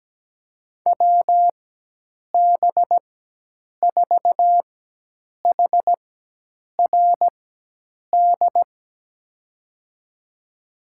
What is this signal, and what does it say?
Morse code, heard as a keyed tone, about 17 wpm, spaced as for 8 wpm, 700 Hz. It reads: WB4HRD
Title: USA Call Signs - 6 Characters - 4x Character Spacing - 17wpm